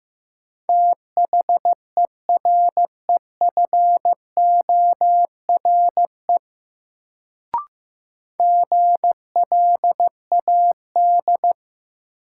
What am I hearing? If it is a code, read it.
Morse code, heard as a keyed tone, about 15 wpm, 700 Hz. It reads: THEREFORE GLAD
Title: Sentences from Top 800 Words - New Word Drill - 15wpm